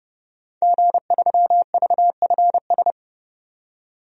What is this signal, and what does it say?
Morse code, heard as a keyed tone, about 29 wpm, 700 Hz. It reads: G3VFS
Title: DXCC-100 Call Signs - 5 Characters - 30wpm